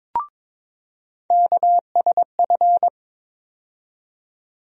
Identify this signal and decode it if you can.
Morse code, heard as a keyed tone, about 22 wpm, 700 Hz. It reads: KSF
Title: USA Call Sign Postfixes - 3 Letters 22wpm